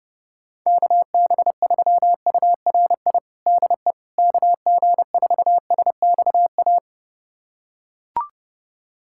Morse code, heard as a keyed tone, about 30 wpm, 700 Hz. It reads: KB3URI DE KG4SXA
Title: USA Callsign DE Callsign 30wpm